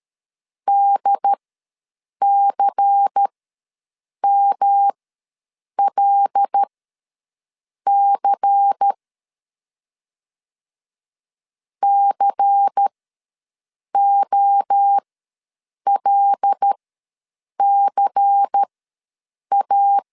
Morse code, heard as a keyed tone, about 13 wpm, spaced as for 7 wpm, 800 Hz. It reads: DCMLC COLCA
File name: LEZIONI / 5 LBVC / 5 LBVC ws 12wpm 3159